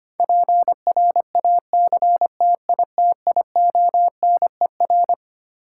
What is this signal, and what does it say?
Morse code, heard as a keyed tone, about 25 wpm, 700 Hz. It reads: PRACTITIONER
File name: Twelve Letter Words - 25wpm